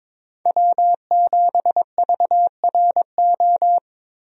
Morse code, heard as a keyed tone, about 22 wpm, 700 Hz. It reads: W7VRO
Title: USA Call Signs - 5 Characters - Repeated 3x - 22wpm